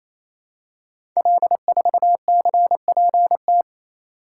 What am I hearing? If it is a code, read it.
Morse code, heard as a keyed tone, about 28 wpm, 700 Hz. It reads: L4CPT
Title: Code Groups - 5 Letter:Numbers - 28wpm